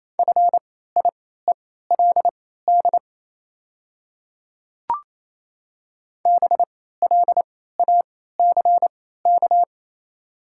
Morse code, heard as a keyed tone, about 28 wpm, spaced as for 16 wpm, 700 Hz. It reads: FIELD BLACK
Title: 3x Character Spacing - Top 500 Words - 28wpm